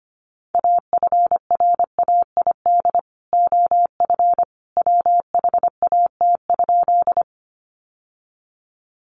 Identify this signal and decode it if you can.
Morse code, heard as a keyed tone, about 25 wpm, 700 Hz. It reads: AFRAID OF WHAT?
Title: Sentences from Top 1500 Words 25wpm